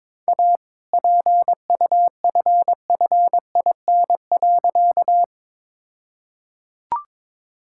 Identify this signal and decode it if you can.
Morse code, heard as a keyed tone, about 23 wpm, 690 Hz. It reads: A PUFFIN.
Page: morse-code-ninja-practice-22wpm